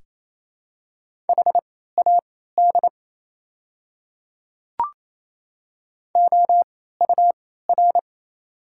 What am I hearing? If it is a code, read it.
Morse code, heard as a keyed tone, about 27 wpm, spaced as for 16 wpm, 690 Hz. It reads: HAD OUR
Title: 3x Character Spacing - Top 200 Words - 28wpm